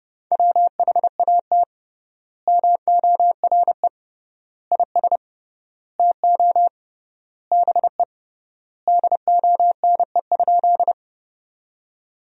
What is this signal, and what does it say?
Morse code, heard as a keyed tone, about 29 wpm, 700 Hz. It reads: WHAT MORE IS TO BE DONE?
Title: 3x Word-Spacing - Sentences from Top 500 Words - 30wpm